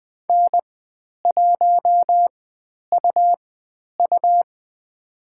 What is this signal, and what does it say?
Morse code, heard as a keyed tone, about 20 wpm, spaced as for 10 wpm, 700 Hz. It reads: N1UU
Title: 109 Lesson 18g - Letter m - Call Signs 20wpm